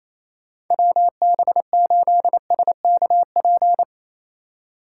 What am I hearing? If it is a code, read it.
Morse code, heard as a keyed tone, about 28 wpm, 700 Hz. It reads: WB8SKP
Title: K1USN SST QSO Practice - 28wpm